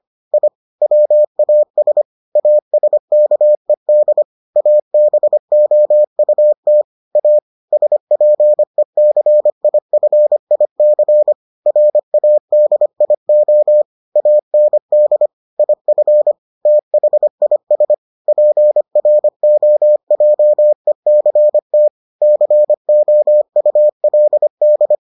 Morse code, heard as a keyed tone, about 25 wpm, 600 Hz. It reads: I WAS ASKED ABOUT A SPECIFIC RADIO AND IF THIS PROJECT COULD